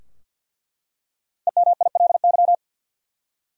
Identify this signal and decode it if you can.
Morse code, heard as a keyed tone, about 48 wpm, 700 Hz. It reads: EMILY
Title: USA Common Names - Max 5 Letters - 50wpm